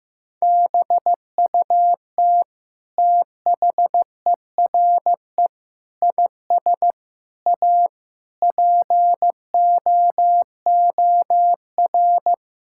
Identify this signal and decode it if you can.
Morse code, heard as a keyed tone, about 15 wpm, 700 Hz. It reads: BUT THERE IS A POOR
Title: Sentences from Top 1000 Words - Review Part 6 - 15wpm